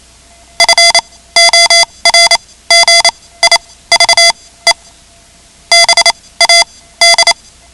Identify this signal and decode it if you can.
Morse code, heard as a keyed tone, about 28 wpm, spaced as for 17 wpm, 2050 Hz. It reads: FORGIVE BAD
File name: ham radio / CodeQuick / Code Quick 10